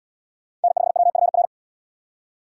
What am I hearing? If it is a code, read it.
Morse code, heard as a keyed tone, about 70 wpm, 700 Hz. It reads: N5WFR